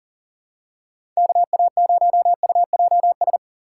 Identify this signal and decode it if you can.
Morse code, heard as a keyed tone, about 37 wpm, 700 Hz. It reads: KA0UJS
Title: USA Call Signs - 6 Characters - 6x Character Spacing - 40wpm